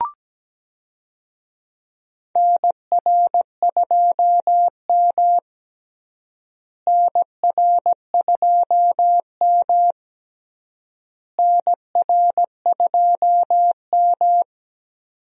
Morse code, heard as a keyed tone, about 17 wpm, 700 Hz. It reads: NR2M NR2M NR2M
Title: USA Call Signs - 4 Characters - Repeated 3x - 17wpm